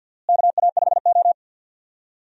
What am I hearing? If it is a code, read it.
Morse code, heard as a keyed tone, about 49 wpm, 700 Hz. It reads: KA5O
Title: USA Call Signs - 4 Characters - Repeated 2x - 50wpm